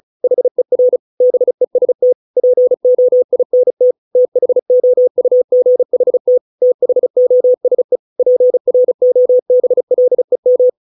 Morse code, heard as a keyed tone, about 34 wpm, 500 Hz. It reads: HER BEST POINT THOUGHT THOSE PROBLEM